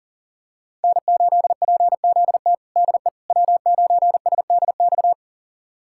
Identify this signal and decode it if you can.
Morse code, heard as a keyed tone, about 40 wpm, 700 Hz. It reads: N8PZT DE W9SDX
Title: USA Callsign DE Callsign - 5 Characters - 40wpm